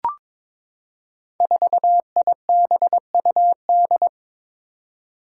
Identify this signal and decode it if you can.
Morse code, heard as a keyed tone, about 22 wpm, 700 Hz. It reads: 4IBUD
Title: Five Challenging Letter-Number Combinations 22wpm